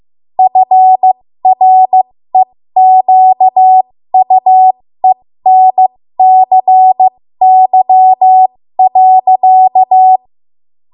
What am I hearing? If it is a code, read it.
Morse code, heard as a keyed tone, about 15 wpm, spaced as for 13 wpm, 750 Hz. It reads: FREQUENCY.
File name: Morse / 13WPM